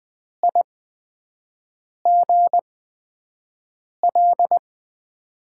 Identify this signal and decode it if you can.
Morse code, heard as a keyed tone, about 20 wpm, spaced as for 5 wpm, 700 Hz. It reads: IGL
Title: USA Call Signs - 8x Character Spacing - 20wpm